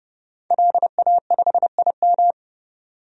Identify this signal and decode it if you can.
Morse code, heard as a keyed tone, about 31 wpm, 690 Hz. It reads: LA5IM
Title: DXCC-100 Call Signs - 5 Characters - 30wpm